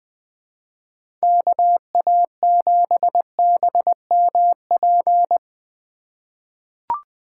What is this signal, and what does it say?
Morse code, heard as a keyed tone, about 21 wpm, 700 Hz. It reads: KA7BMP
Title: USA Call Signs - 6 Characters - 8x Character Spacing - 20wpm